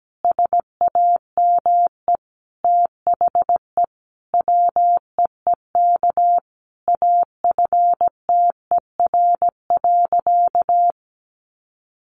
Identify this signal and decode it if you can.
Morse code, heard as a keyed tone, about 17 wpm, 700 Hz. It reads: SAME THE WEEK AFTER.
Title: Sentences from Top 500 Words 17wpm